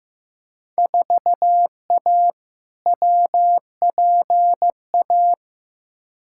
Morse code, heard as a keyed tone, about 15 wpm, 700 Hz. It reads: 4A WPA
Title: Field Day Exchanges Encore - 15wpm